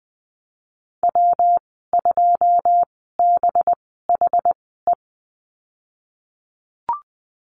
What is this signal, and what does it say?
Morse code, heard as a keyed tone, about 20 wpm, spaced as for 15 wpm, 700 Hz. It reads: W2BHE
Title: 186 Lesson 30g - Letter b - Call Signs 20wpm